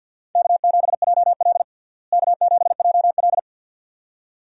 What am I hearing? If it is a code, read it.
Morse code, heard as a keyed tone, about 52 wpm, 700 Hz. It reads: K7JL K7JL
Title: USA Call Signs - 4 Characters - Repeated 2x - 50wpm